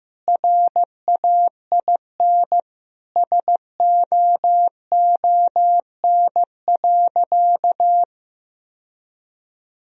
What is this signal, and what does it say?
Morse code, heard as a keyed tone, about 15 wpm, 700 Hz. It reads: RAIN SOON.
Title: Sentences from Top 1000 Words - Review Part 1 - 15wpm